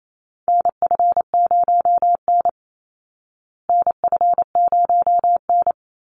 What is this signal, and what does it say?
Morse code, heard as a keyed tone, about 28 wpm, 700 Hz. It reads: NF0N NF0N
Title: Call Signs Repeated 3x - 28wpm